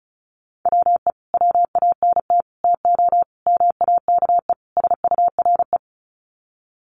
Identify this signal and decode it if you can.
Morse code, heard as a keyed tone, about 34 wpm, 700 Hz. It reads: WE WANT TO MAKE SURE
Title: Sets of 5 Words Encore2 35wpm